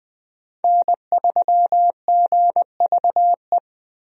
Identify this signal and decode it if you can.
Morse code, heard as a keyed tone, about 20 wpm, 700 Hz. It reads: N3GVE
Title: USA Call Signs - 5 Characters - Repeated 3x - 20wpm